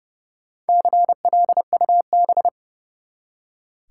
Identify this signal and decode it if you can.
Morse code, heard as a keyed tone, about 31 wpm, 700 Hz. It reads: CLUB
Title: Four Letter Words - 30wpm